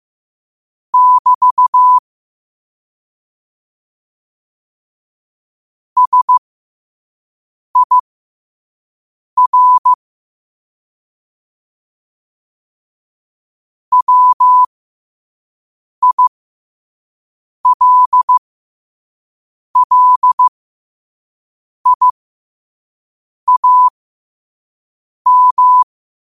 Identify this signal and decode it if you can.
Morse code, heard as a keyed tone, about 15 wpm, spaced as for 5 wpm, 1000 Hz. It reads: <BT> SIR WILLIAM
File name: qotdfiles / 5wpm